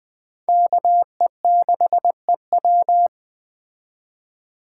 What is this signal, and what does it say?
Morse code, heard as a keyed tone, about 20 wpm, 700 Hz. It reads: KE6EW
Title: Speed Racing - USA 2x2 Call Signs 17wpm